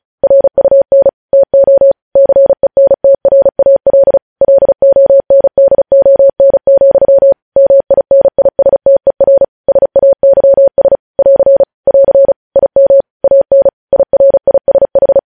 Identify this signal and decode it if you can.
Morse code, heard as a keyed tone, about 36 wpm, 550 Hz. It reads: RUN TO CENTRAL LONDON, MINISTER SAYS <AR> <AR> IM AN IRISH